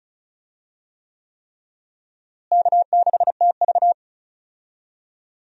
Morse code, heard as a keyed tone, about 36 wpm, 700 Hz. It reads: K6TV
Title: USA Call Signs - 4 Characters - Repeated 3x - 35wpm